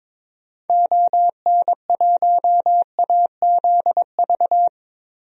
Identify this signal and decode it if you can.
Morse code, heard as a keyed tone, about 22 wpm, 700 Hz. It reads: ON1AZV
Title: DXCC-100 Call Signs - 6 Characters - 22wpm